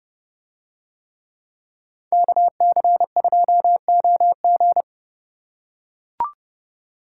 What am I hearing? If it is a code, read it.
Morse code, heard as a keyed tone, about 29 wpm, 700 Hz. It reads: KC2OG